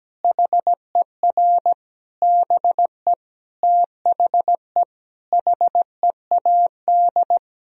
Morse code, heard as a keyed tone, about 17 wpm, 700 Hz. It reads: HER BE THE HEAD
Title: Sentences from Top 1000 Words - Review Part 1 - 17wpm